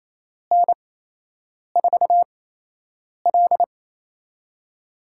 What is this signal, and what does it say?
Morse code, heard as a keyed tone, about 27 wpm, spaced as for 8 wpm, 700 Hz. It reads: N4L
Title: USA Call Signs - 4 Characters - 8x Character Spacing - 28wpm